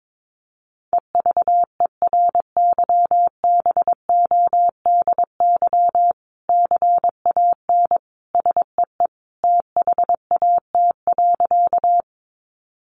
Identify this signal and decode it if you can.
Morse code, heard as a keyed tone, about 22 wpm, 700 Hz. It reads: EVERYBODY CAN SEE THAT.
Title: Sentences from Top 1400 Words 22wpm